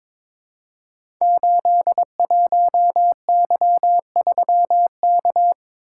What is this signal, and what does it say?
Morse code, heard as a keyed tone, about 22 wpm, 700 Hz. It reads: 81Y3K
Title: Code Groups - 5 Letter:Numbers - 22wpm